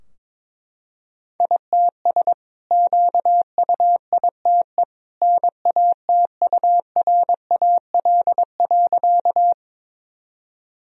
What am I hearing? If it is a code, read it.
Morse code, heard as a keyed tone, about 22 wpm, 700 Hz. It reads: ITS QUITE NATURAL.
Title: Sentences from Top 1000 Words - Review Part 4 - 22wpm